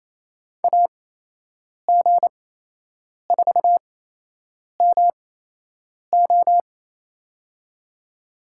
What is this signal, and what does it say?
Morse code, heard as a keyed tone, about 28 wpm, spaced as for 8 wpm, 700 Hz. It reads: AG4MO
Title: USA Call Signs - 5 Characters - 8x Character Spacing - 28wpm